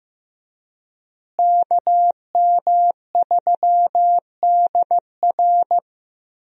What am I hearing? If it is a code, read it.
Morse code, heard as a keyed tone, about 15 wpm, 700 Hz. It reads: KM3DR
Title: USA Call Signs - 5 Characters - Repeated 3x - 15wpm